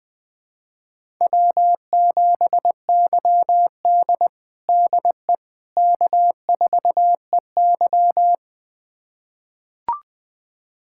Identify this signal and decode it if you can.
Morse code, heard as a keyed tone, about 20 wpm, 700 Hz. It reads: W7YD DE K4EY
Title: USA Callsign DE Callsign - 4 Characters - 20wpm